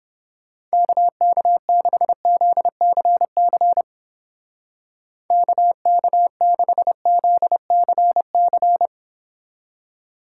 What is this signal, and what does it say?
Morse code, heard as a keyed tone, about 30 wpm, 700 Hz. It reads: KK6ZCC KK6ZCC